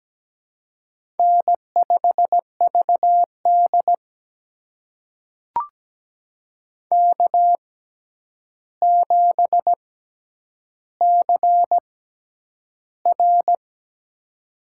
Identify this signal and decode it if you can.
Morse code, heard as a keyed tone, about 17 wpm, spaced as for 9 wpm, 700 Hz. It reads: N5VD K7CR
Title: USA Call Signs - 4 Characters - 6x Character Spacing - 17wpm